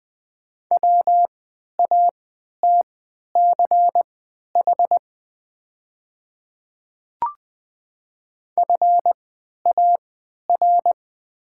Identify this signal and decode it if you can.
Morse code, heard as a keyed tone, about 20 wpm, spaced as for 11 wpm, 690 Hz. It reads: WATCH FAR
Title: 3x Character Spacing - Top 300 Words - 20wpm